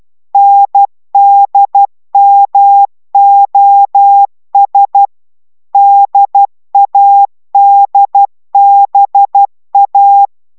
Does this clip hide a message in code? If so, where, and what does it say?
Yes: Morse code from 0.3 to 10.3 s, heard as a keyed tone, about 12 wpm, 800 Hz. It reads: NDMOS DADBA